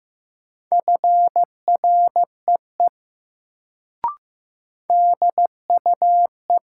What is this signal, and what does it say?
Morse code, heard as a keyed tone, about 15 wpm, 700 Hz. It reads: FREE DUE
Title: Sets of 5 Words Encore - New Word Drill - 15wpm